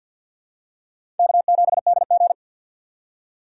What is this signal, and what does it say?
Morse code, heard as a keyed tone, about 53 wpm, 700 Hz. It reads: K7DG
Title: USA Call Signs - 6x Character Spacing - 50wpm